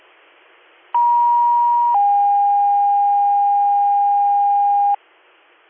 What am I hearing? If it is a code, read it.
Two-tone page, heard as a one-second tone, then a three-second tone, about 950 then 810 Hz, with noise. It is heard from 0.9 to 5.0 s.